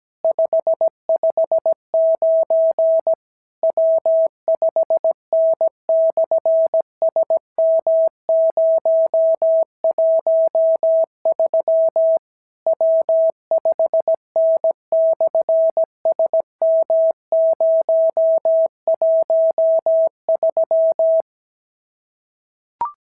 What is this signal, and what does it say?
Morse code, heard as a keyed tone, about 17 wpm, 650 Hz. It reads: 559 W5N/SM013 W5N/SM013
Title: SOTA QSO Practice 17wpm